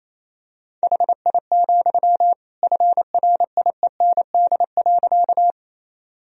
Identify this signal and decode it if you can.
Morse code, heard as a keyed tone, about 28 wpm, 700 Hz. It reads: HI, FRIEND.
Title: Sentences from Top 1000 Words - Review Part 3 - 28wpm